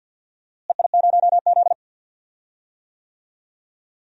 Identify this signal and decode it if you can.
Morse code, heard as a keyed tone, about 49 wpm, 700 Hz. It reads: EI0Z
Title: DXCC-100 Call Signs - 4 Characters - Repeated 2x - 50wpm